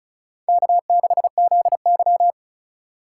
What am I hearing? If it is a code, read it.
Morse code, heard as a keyed tone, about 36 wpm, 700 Hz. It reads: K6ZY
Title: USA Call Signs - 4 Characters - 3x Character Spacing - 35wpm